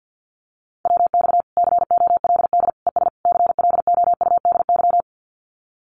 Morse code, heard as a keyed tone, about 50 wpm, 700 Hz. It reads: WX COLD ES CLOUDY
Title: Common QSO Phrases 50wpm